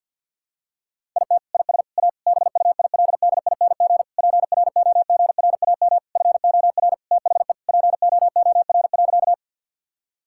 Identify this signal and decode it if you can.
Morse code, heard as a keyed tone, about 51 wpm, 700 Hz. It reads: IT IS A BUILDING PROGRAM FOR THE POOR.
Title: Sentences from Top 1000 Words - Review Part 5 - 50wpm